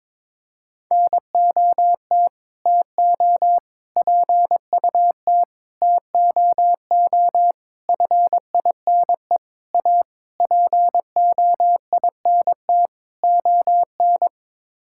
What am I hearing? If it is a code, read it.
Morse code, heard as a keyed tone, about 22 wpm, 700 Hz. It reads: NOT TO PUT TOO FINE A POINT ON
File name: practice / 22wpm / Sets of 8 Words - 22wpm